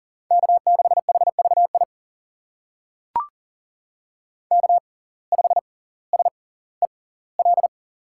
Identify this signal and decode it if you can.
Morse code, heard as a keyed tone, about 40 wpm, spaced as for 20 wpm, 700 Hz. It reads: K6HVI K5SEL
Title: USA Call Signs - 5 Characters - 6x Character Spacing - 40wpm